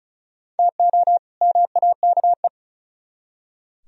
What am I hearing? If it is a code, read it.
Morse code, heard as a keyed tone, about 47 wpm, spaced as for 41 wpm, 700 Hz. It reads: TO MAKE
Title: Speed-Racing - Sets of 2 Words 35wpm